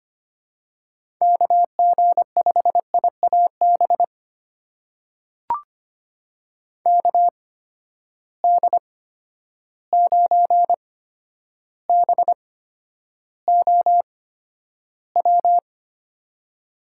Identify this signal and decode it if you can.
Morse code, heard as a keyed tone, about 25 wpm, spaced as for 11 wpm, 700 Hz. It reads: KG5IAB KD9BOW
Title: USA Call Signs - 6 Characters - 8x Character Spacing - 25wpm